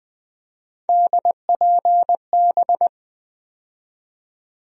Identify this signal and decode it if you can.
Morse code, heard as a keyed tone, about 21 wpm, 700 Hz. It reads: DPB
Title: USA Call Sign Postfixes - 3 Letters 20wpm